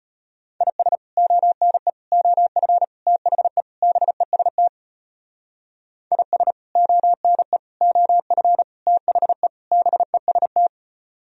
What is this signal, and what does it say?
Morse code, heard as a keyed tone, about 33 wpm, 700 Hz. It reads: IS ONE OF THE BEST IS ONE OF THE BEST